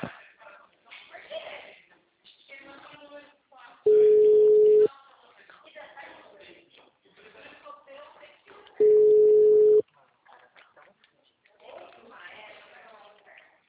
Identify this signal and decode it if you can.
Ringback tone (Europe, 425 Hz), with speech.
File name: out- -2015- - - .471